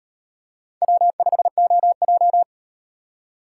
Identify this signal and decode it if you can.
Morse code, heard as a keyed tone, about 38 wpm, 700 Hz. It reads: W5OJ